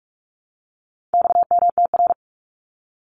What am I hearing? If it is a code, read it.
Morse code, heard as a keyed tone, about 44 wpm, 700 Hz. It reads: XMTR
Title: Abbreviations 45wpm